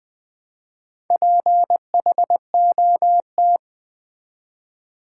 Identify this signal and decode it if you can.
Morse code, heard as a keyed tone, about 20 wpm, 700 Hz. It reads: PHOT
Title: Build A Word - 1 Letter At A Time - 3 to 5-Letter Words - 20wpm